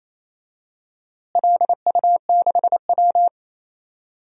Morse code, heard as a keyed tone, about 28 wpm, 700 Hz. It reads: LU6W
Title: DXCC-100 Call Signs - 4 Characters - Repeated 3x - 28wpm